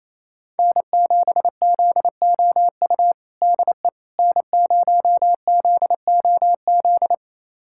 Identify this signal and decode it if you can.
Morse code, heard as a keyed tone, about 28 wpm, 700 Hz. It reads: N7ZOU DE N0ZOZ
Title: USA Callsign DE Callsign - 5 Characters - 28wpm